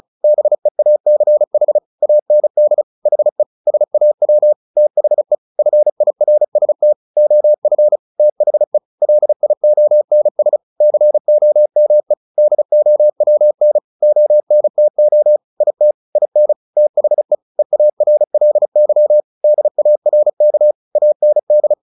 Morse code, heard as a keyed tone, about 34 wpm, 600 Hz. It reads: BEACH AND HE SAW THE FIRST OF THE LIONS COME DOWN ONTO IT IN THE EARLY DARK AND
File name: the old man and the sea 2 35wpm